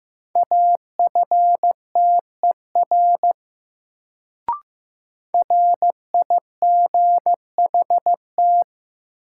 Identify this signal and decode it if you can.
Morse code, heard as a keyed tone, about 15 wpm, 700 Hz. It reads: AFTER RIGHT
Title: Sentences from Top 200 Words - New Word Drill - 15wpm